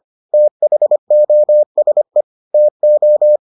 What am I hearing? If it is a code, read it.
Morse code, heard as a keyed tone, about 25 wpm, 600 Hz. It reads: THOSE TO